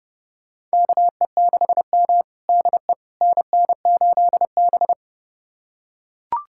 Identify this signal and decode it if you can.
Morse code, heard as a keyed tone, about 30 wpm, 700 Hz. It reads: KE6M DE NN8B
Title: USA Callsign DE Callsign - 4 Characters - 30wpm